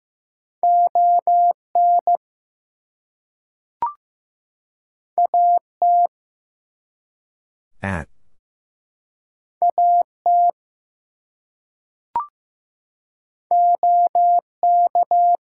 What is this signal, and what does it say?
Morse code, heard as a keyed tone, about 15 wpm, 700 Hz. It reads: ON AT AT OK
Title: Two Letter Words - 15wpm